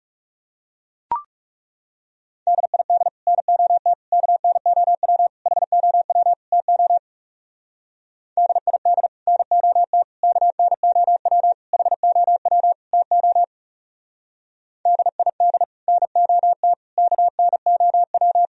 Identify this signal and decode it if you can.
Morse code, heard as a keyed tone, about 35 wpm, 700 Hz. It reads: DID NOT KNOW HOW TO DID NOT KNOW HOW TO DID NOT KNOW
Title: Speed-Racing - Sets of 5 Words 30wpm